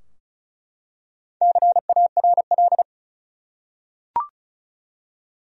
Morse code, heard as a keyed tone, about 36 wpm, 700 Hz. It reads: CARL
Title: USA Common Names - Max 4 Letters - 35wpm